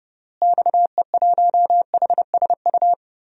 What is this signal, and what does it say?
Morse code, heard as a keyed tone, about 32 wpm, 700 Hz. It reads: XE1HSU
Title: DXCC-100 Call Signs - 6 Characters - Repeated 3x - 30wpm